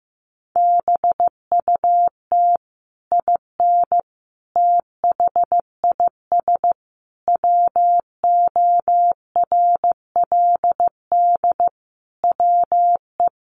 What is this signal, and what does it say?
Morse code, heard as a keyed tone, about 15 wpm, 700 Hz. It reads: BUT IN THIS WORLD WE
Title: Sentences from Top 600 Words 15wpm